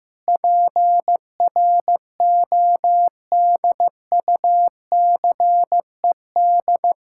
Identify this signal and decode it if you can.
Morse code, heard as a keyed tone, about 15 wpm, 700 Hz. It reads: PRODUCED
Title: Sentences from Top 1000 Words - Review Part 6 - 15wpm